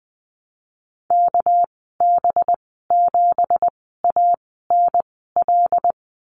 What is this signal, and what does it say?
Morse code, heard as a keyed tone, about 20 wpm, spaced as for 14 wpm, 700 Hz. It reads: KB7ANL
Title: 245 Lesson 39g - Letter z - Call Signs 20wpm